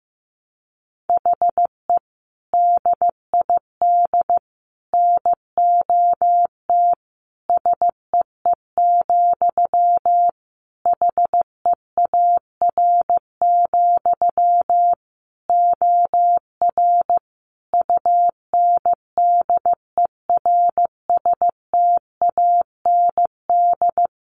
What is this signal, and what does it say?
Morse code, heard as a keyed tone, about 15 wpm, 700 Hz. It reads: HE DID NOT SEE, HEAR, OR UNDERSTAND
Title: Sentences from Top 1300 Words 15wpm